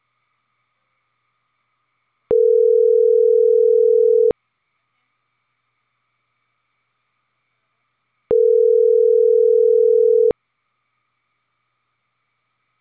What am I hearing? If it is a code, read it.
Ringback tone (North America), 440 + 480 Hz.